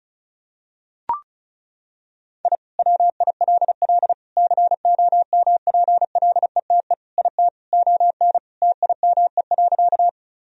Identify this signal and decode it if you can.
Morse code, heard as a keyed tone, about 34 wpm, 700 Hz. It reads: I WILL COMPLETE IT ON TIME.
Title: Sentences from Top 1000 Words - Review Part 2 - 35wpm